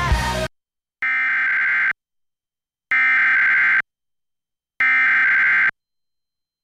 Emergency Alert System (SAME) tones (header bursts), heard as harsh buzzing bursts. It is heard from 1.0 to 5.7 s.